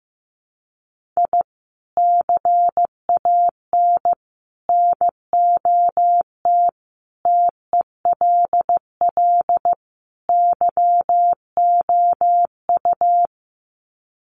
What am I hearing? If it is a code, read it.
Morse code, heard as a keyed tone, about 15 wpm, 700 Hz. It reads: I CAN NOT TELL YOU
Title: Sets of 5 Words 15wpm